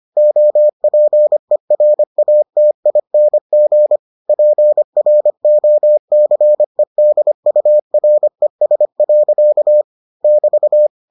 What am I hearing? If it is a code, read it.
Morse code, heard as a keyed tone, about 25 wpm, 600 Hz. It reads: OPERATING PROCEDURES. <BT>